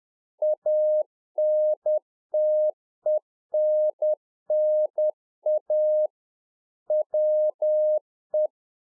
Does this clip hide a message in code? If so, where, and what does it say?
Yes: Morse code from 0.4 to 8.5 s, heard as a keyed tone, about 10 wpm, 610 Hz. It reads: ANTENNA WE